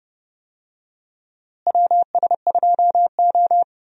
Morse code, heard as a keyed tone, about 30 wpm, 700 Hz. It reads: WS2O
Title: USA Call Signs - 4 Characters - Repeated 3x - 30wpm